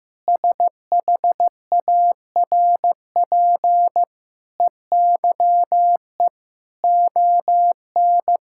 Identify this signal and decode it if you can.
Morse code, heard as a keyed tone, about 16 wpm, 700 Hz. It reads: SHARP EYE ON
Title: Sentences from Top 1600 Words 15wpm